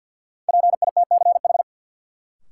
Morse code, heard as a keyed tone, about 51 wpm, 700 Hz. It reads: PITCH
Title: Sentences from Top 1300 Words - New Word Drill - 50wpm